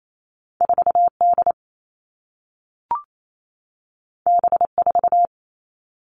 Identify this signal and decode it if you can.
Morse code, heard as a keyed tone, about 28 wpm, 700 Hz. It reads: 4D B4
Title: All Cut Numbers - 2-Digit 28wpm